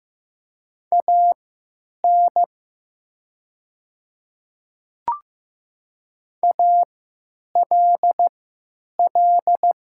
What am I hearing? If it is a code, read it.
Morse code, heard as a keyed tone, about 15 wpm, spaced as for 9 wpm, 700 Hz. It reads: AN ALL
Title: 3x Character Spacing - Top 100 Words - 15wpm